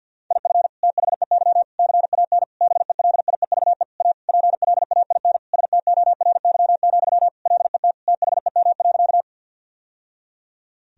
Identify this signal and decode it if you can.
Morse code, heard as a keyed tone, about 50 wpm, 700 Hz. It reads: IF THEY CAN BELIEVE A PLAIN STORY, LET THEM.
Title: Sentences from Top 1000 Words - Review Part 2 - 50wpm